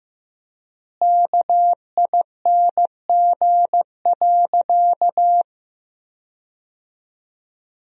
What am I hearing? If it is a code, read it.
Morse code, heard as a keyed tone, about 15 wpm, 700 Hz. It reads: KING.
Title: 3x Word-Spacing - Sentences from Top 400 Words - 15wpm